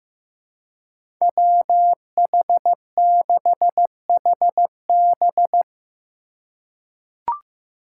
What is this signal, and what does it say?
Morse code, heard as a keyed tone, about 15 wpm, 700 Hz. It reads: WH6HB
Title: USA Call Signs - 5 Characters - 3x Character Spacing - 15wpm